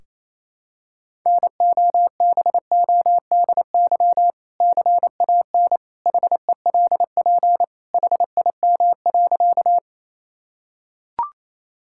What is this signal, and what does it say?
Morse code, heard as a keyed tone, about 28 wpm, 700 Hz. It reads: NOBODY CAN HELP HIM.
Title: Sentences from Top 1600 Words 28wpm